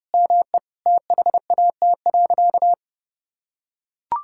Morse code, heard as a keyed tone, about 32 wpm, 700 Hz. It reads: ME THAT.
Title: Sentences from Top 1600 Words 30wpm